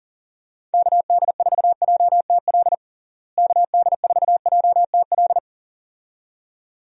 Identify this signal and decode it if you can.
Morse code, heard as a keyed tone, about 40 wpm, 700 Hz. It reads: KD4JTL KD4JTL
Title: USA Call Signs - 6 Characters - Repeated 2x - 40wpm